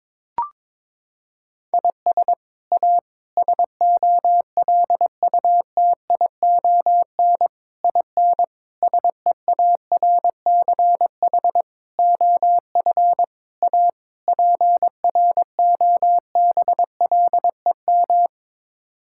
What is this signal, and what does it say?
Morse code, heard as a keyed tone, about 22 wpm, 700 Hz. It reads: IS A SOLUTION IN SEARCH OF A PROBLEM
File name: practice / 22wpm / Sets of 8 Words - 22wpm